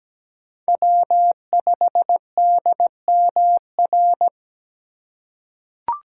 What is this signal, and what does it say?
Morse code, heard as a keyed tone, about 17 wpm, 700 Hz. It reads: W5DMR
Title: USA Call Signs - 5 Characters - 3x Character Spacing - 17wpm